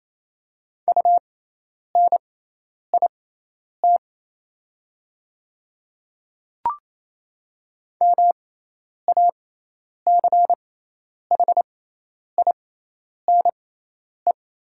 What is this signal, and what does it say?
Morse code, heard as a keyed tone, about 28 wpm, spaced as for 10 wpm, 700 Hz. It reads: UNIT MACHINE
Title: 6x Character Spacing - Top 500 Words - 28wpm